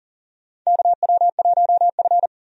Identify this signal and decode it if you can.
Morse code, heard as a keyed tone, about 40 wpm, 700 Hz. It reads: KW1F
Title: USA Call Signs - 4 Characters - 40wpm